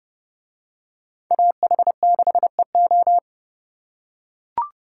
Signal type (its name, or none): Morse code